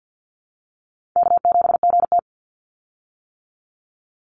Morse code, heard as a keyed tone, about 50 wpm, 700 Hz. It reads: K7GT
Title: Call Signs - 50wpm